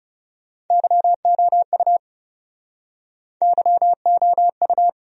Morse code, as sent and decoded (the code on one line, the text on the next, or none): -.-- --- ..- / -.-- --- ..-
YOU YOU